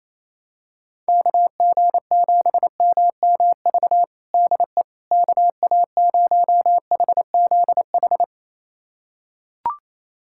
Morse code, heard as a keyed tone, about 28 wpm, 700 Hz. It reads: KG7MMV DE KA0HZH
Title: USA Callsign DE Callsign 28wpm